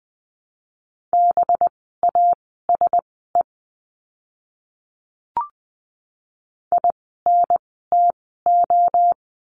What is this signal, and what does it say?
Morse code, heard as a keyed tone, about 20 wpm, spaced as for 14 wpm, 700 Hz. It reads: BASE INTO
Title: 185 Lesson 30f - Letter b - Words 20wpm